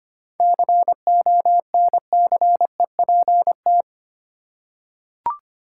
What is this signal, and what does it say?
Morse code, heard as a keyed tone, about 25 wpm, 700 Hz. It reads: CONCEPT
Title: Seven Letter Words - 25wpm